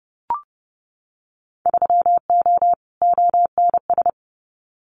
Morse code, heard as a keyed tone, about 30 wpm, 700 Hz. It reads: 3O ONS
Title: Winter Field Day Exchanges 30wpm